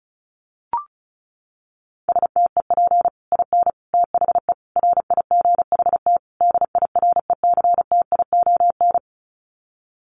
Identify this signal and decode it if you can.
Morse code, heard as a keyed tone, about 34 wpm, 700 Hz. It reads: STEP IN THE RIGHT DIRECTION
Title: Sets of 5 Words Encore 35wpm